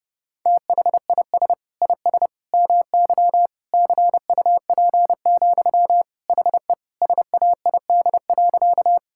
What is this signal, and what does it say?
Morse code, heard as a keyed tone, about 32 wpm, 700 Hz. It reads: THIS IS MY CUP, HE SAID.
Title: Sentences from Top 1600 Words 30wpm